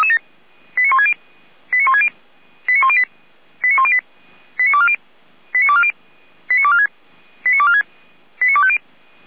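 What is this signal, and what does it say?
Five-tone selective call (ZVEI), heard as quick runs of five tones, tones of about 70 ms. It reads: 89160 89160 89108 89108 89240 89240 89246 89246 89250
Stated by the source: Received by SDR; ZVEI five-tone selective call.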